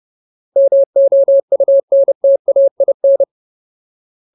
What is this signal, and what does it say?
Morse code, heard as a keyed tone, about 30 wpm, 550 Hz. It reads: MOUNTAIN
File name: Ultimate Word List / 30WPM Ultimate